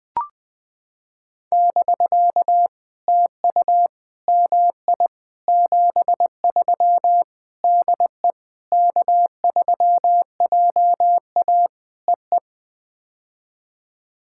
Morse code, heard as a keyed tone, about 20 wpm, 700 Hz. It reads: <BK> TU MI 73 DE K3JA EE
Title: POTA QSO Practice - 20wpm